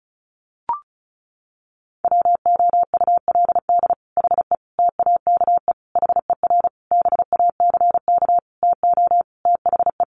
Morse code, heard as a keyed tone, about 34 wpm, 700 Hz. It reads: WOULD HE TAKE HER BACK TO THE
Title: Sentences from Top 1500 Words 35wpm